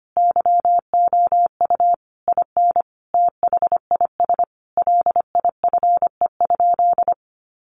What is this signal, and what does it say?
Morse code, heard as a keyed tone, about 25 wpm, 700 Hz. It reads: YOU IN THIS LIFE?
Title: Sentences from Top 1200 Words 25wpm